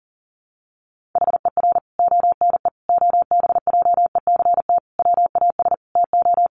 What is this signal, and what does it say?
Morse code, heard as a keyed tone, about 40 wpm, 700 Hz. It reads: HER ONE OBJECT WAS TO
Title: Sentences from Top 1500 Words 40wpm